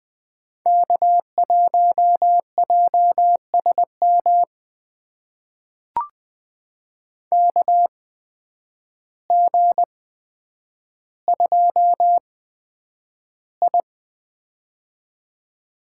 Morse code, heard as a keyed tone, about 20 wpm, spaced as for 9 wpm, 700 Hz. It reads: K1JSM KG2I
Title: USA Call Signs - 8x Character Spacing - 20wpm